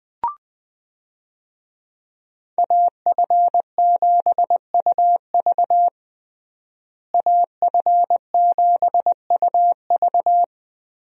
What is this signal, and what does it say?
Morse code, heard as a keyed tone, about 20 wpm, 700 Hz. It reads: AF7UV AF7UV
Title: USA Call Signs - 5 Characters - Repeated 2x - 20wpm